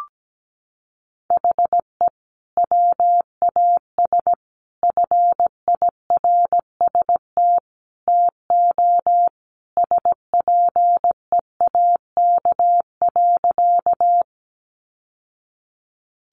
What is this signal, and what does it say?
Morse code, heard as a keyed tone, about 17 wpm, 700 Hz. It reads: HE WAS FIRST TO SPEAK.
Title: Sentences from Top 1100 Words 17wpm